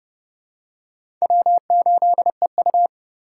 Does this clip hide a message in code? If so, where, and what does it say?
Yes: Morse code from 1.2 to 2.9 s, heard as a keyed tone, about 26 wpm, 700 Hz. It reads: W8EU